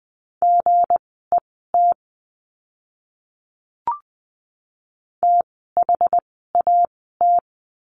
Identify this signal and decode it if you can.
Morse code, heard as a keyed tone, about 20 wpm, spaced as for 15 wpm, 700 Hz. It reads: GET THAT
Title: 230 Lesson 37f - Letter x - Words 20wpm